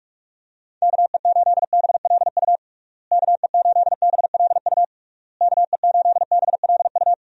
Morse code, heard as a keyed tone, about 46 wpm, 700 Hz. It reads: KE8BLU KE8BLU KE8BLU
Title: USA Call Signs - 6 Characters - Repeated 3x - 45wpm